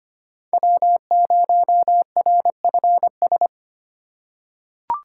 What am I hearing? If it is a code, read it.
Morse code, heard as a keyed tone, about 25 wpm, 700 Hz. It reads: W0RFS
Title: USA Call Signs - 2x Character Spacing - 25wpm